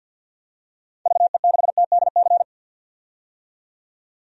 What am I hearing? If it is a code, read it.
Morse code, heard as a keyed tone, about 51 wpm, 700 Hz. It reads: VE6TDC
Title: DXCC-100 Call Signs - 6 Characters - 50wpm